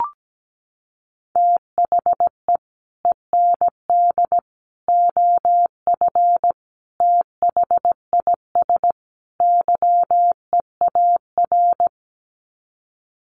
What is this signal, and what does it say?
Morse code, heard as a keyed tone, about 17 wpm, 700 Hz. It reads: THE END OF THIS YEAR
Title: Sets of 5 Words Encore 17wpm